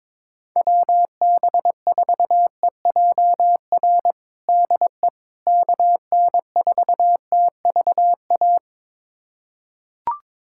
Morse code, heard as a keyed tone, about 22 wpm, 700 Hz. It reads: WB4EJR DE KN4TVA
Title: USA Callsign DE Callsign 22wpm